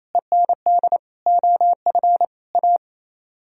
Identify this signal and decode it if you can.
Morse code, heard as a keyed tone, about 34 wpm, 700 Hz. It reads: END OF A
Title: Speed-Racing - Sets of 4 Words 28wpm